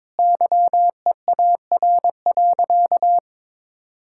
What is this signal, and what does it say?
Morse code, heard as a keyed tone, about 22 wpm, 700 Hz. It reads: YEAR.
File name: practice / 22wpm / Sentences with 3-Word Phrases - 22wpm